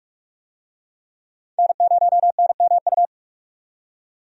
Morse code, heard as a keyed tone, about 46 wpm, 700 Hz. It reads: N0NMU
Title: USA Call Signs - 5 Characters - 45wpm